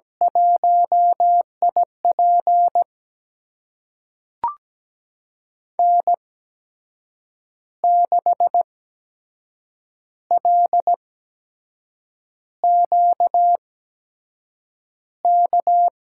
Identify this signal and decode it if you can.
Morse code, heard as a keyed tone, about 17 wpm, spaced as for 6 wpm, 700 Hz. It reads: 1IP N6LQK
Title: USA Call Signs - 8x Character Spacing - 17wpm